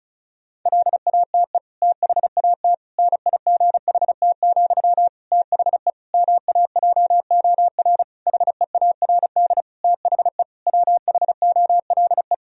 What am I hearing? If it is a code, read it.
Morse code, heard as a keyed tone, about 36 wpm, 700 Hz. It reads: LATE THAT NIGHT, THE MAJOR HEARD THE WHOLE
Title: Sentences from Top 1000 Words - Review Part 4 - 35wpm